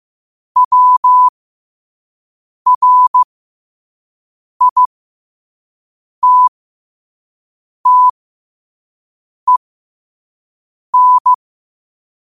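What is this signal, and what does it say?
Morse code, heard as a keyed tone, about 15 wpm, spaced as for 5 wpm, 1000 Hz. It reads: WRITTEN
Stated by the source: 5 wpm Morse code practice recording.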